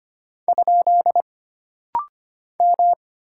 Morse code, heard as a keyed tone, about 25 wpm, 700 Hz. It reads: ? M
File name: Single Character (Letter:Number:Punctuation:Prosign) - Rapid-Fire - 25wpm